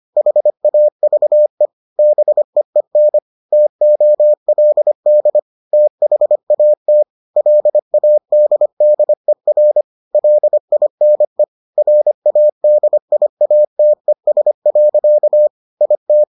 Morse code, heard as a keyed tone, about 25 wpm, 600 Hz. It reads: HAVE BEEN TOLD THAT LADDER LINE RADIATES. IT